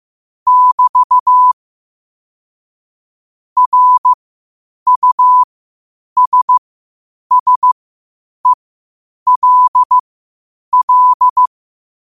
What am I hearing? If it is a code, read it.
Morse code, heard as a keyed tone, about 15 wpm, spaced as for 8 wpm, 1000 Hz. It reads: <BT> RUSSELL
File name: qotdfiles / 8wpm